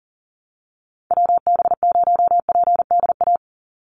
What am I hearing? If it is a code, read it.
Morse code, heard as a keyed tone, about 40 wpm, 700 Hz. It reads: WB0PDA
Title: USA Call Signs 40wpm